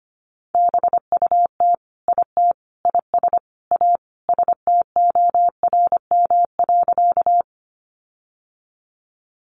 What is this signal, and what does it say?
Morse code, heard as a keyed tone, about 25 wpm, 700 Hz. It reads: BUT IT IS A STORM.
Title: Sentences from Top 1500 Words 25wpm